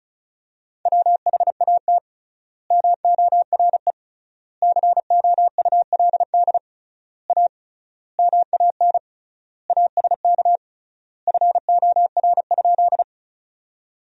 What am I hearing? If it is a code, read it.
Morse code, heard as a keyed tone, about 34 wpm, 700 Hz. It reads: WHAT MORE COULD A MAN ASK FOR?
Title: 3x Word-Spacing - Sentences from Top 200 Words - 35wpm